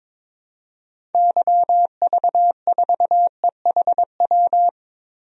Morse code, heard as a keyed tone, about 22 wpm, 700 Hz. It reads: YV4EHW
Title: DXCC-100 Call Signs - 6 Characters - 22wpm